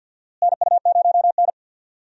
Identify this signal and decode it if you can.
Morse code, heard as a keyed tone, about 47 wpm, 700 Hz. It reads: NU0N
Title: USA Call Signs - 4 Characters - Repeated 2x - 50wpm